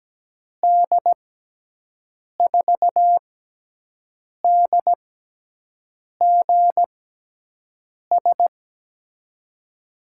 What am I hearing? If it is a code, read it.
Morse code, heard as a keyed tone, about 17 wpm, spaced as for 6 wpm, 700 Hz. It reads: D4DGS
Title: USA Call Signs - 6 Characters - 6x Character Spacing - 17wpm